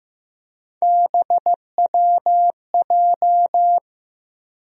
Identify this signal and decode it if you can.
Morse code, heard as a keyed tone, about 15 wpm, 700 Hz. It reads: BWJ
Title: USA Call Sign Postfixes - 3 Letters 15wpm